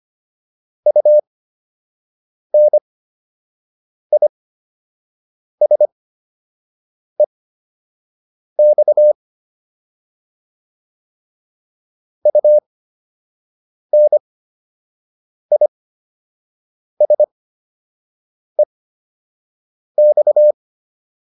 Morse code, heard as a keyed tone, about 25 wpm, spaced as for 6 wpm, 600 Hz. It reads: UNISEX UNISEX